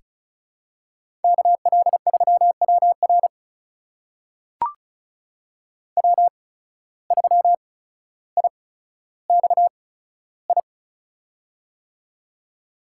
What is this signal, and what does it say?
Morse code, heard as a keyed tone, about 36 wpm, spaced as for 15 wpm, 700 Hz. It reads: KL3WR W3IXI
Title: USA Call Signs - 5 Characters - 8x Character Spacing - 35wpm